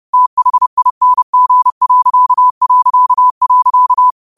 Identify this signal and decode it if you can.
Morse code, heard as a keyed tone, about 30 wpm, 1000 Hz. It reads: THING...